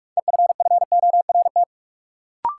Morse code, heard as a keyed tone, about 46 wpm, 700 Hz. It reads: EFFORT
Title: Sentences from Top 1200 Words - New Word Drill Alt - 45wpm